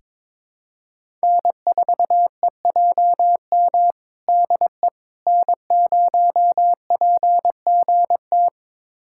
Morse code, heard as a keyed tone, about 22 wpm, 700 Hz. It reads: N4EJM DE N0PGT
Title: USA Callsign DE Callsign - 5 Characters - 22wpm